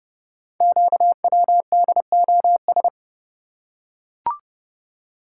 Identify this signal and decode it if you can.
Morse code, heard as a keyed tone, about 31 wpm, 700 Hz. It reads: QWDOS
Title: Code Groups - 5 Letters - 30wpm